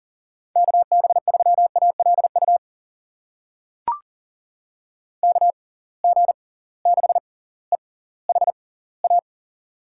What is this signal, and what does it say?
Morse code, heard as a keyed tone, about 40 wpm, spaced as for 21 wpm, 700 Hz. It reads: KB3ALU KG6EHA
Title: USA Call Signs - 6 Characters - 6x Character Spacing - 40wpm